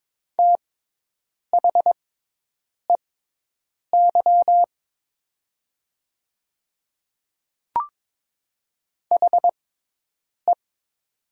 Morse code, heard as a keyed tone, about 23 wpm, spaced as for 8 wpm, 700 Hz. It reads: THEY HE
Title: 6x Character Spacing - Top 100 Words - 22wpm